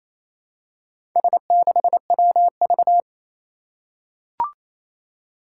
Morse code, heard as a keyed tone, about 27 wpm, 700 Hz. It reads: S6WV
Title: Four Challenging Letter-Number Combinations 28wpm